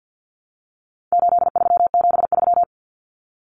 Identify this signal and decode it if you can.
Morse code, heard as a keyed tone, about 50 wpm, 700 Hz. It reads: 8373
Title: Numbers - 4-Digit 50wpm